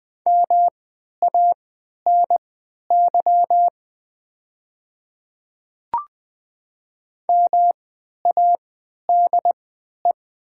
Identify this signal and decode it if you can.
Morse code, heard as a keyed tone, about 20 wpm, spaced as for 11 wpm, 700 Hz. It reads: MANY MADE
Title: 3x Character Spacing - Top 200 Words - 20wpm